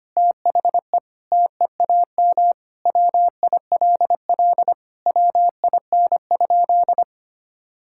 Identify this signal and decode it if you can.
Morse code, heard as a keyed tone, about 25 wpm, 700 Hz. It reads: THE TEAM WILL WIN?
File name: practice / 25wpm / Sentences with 4-Word Phrases - 25wpm